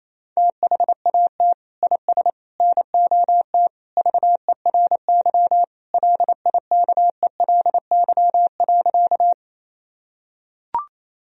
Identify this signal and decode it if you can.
Morse code, heard as a keyed tone, about 28 wpm, 700 Hz. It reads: THAT IS NOT VERY LIKELY.
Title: Sentences from Top 1000 Words - Review Part 4 - 28wpm